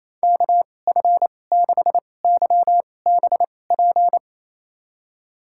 Morse code, heard as a keyed tone, about 28 wpm, spaced as for 20 wpm, 700 Hz. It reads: KF6YBP
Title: USA Call Signs - 6 Characters - 2x Character Spacing - 28wpm